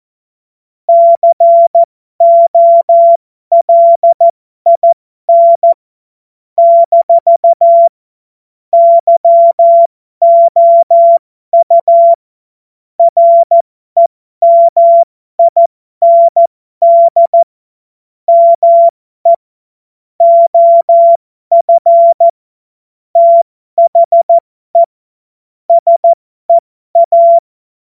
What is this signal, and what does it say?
Morse code, heard as a keyed tone, about 14 wpm, spaced as for 12 wpm, 680 Hz. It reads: COLIN - YOU REMIND ME OF THE SEA
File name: Morsekurs / Witze